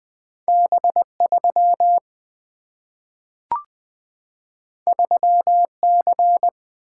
Morse code, heard as a keyed tone, about 20 wpm, 700 Hz. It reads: B3 3C
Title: Single-Letter Single-Number 20wpm